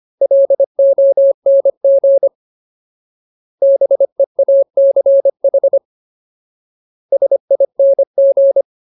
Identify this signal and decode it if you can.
Morse code, heard as a keyed tone, about 25 wpm, 550 Hz. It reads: LONG BEACH SING
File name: Ultimate Word List / 25WPM Ultimate